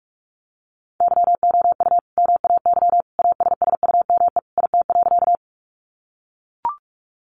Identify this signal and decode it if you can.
Morse code, heard as a keyed tone, about 44 wpm, 700 Hz. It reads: YOU MAY ASSUME IT.
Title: Sentences from Top 1500 Words 45wpm